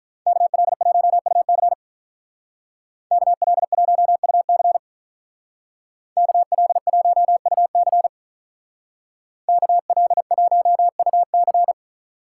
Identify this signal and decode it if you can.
Morse code, heard as a keyed tone, about 43 wpm, 700 Hz. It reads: KL1UC KL1UC KL1UC KL1UC
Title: Speed Racing - USA 2x2 Call Signs 35wpm